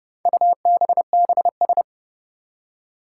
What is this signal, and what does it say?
Morse code, heard as a keyed tone, about 31 wpm, 700 Hz. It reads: UBBS
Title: Four Challenging Letter-Number Combinations 30wpm